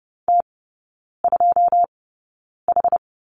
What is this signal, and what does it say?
Morse code, heard as a keyed tone, about 30 wpm, spaced as for 9 wpm, 700 Hz. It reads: T2H
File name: Lesson 16e - Letter c - Three Characters 30wpm